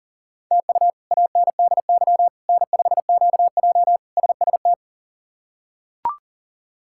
Morse code, heard as a keyed tone, about 40 wpm, 690 Hz. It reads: TU ANDY N5QJ SST
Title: K1USN SST QSO Practice - 40wpm